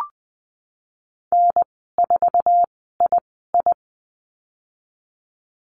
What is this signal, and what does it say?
Morse code, heard as a keyed tone, about 20 wpm, spaced as for 15 wpm, 700 Hz. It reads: N4II
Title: 048 Lesson 9g - Number 4 - Call Signs 20wpm